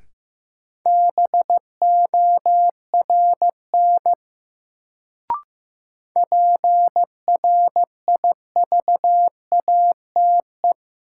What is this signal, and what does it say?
Morse code, heard as a keyed tone, about 15 wpm, 700 Hz. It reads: BORN PRIVATE
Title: Sentences from Top 1000 Words - New Word Drill - 15wpm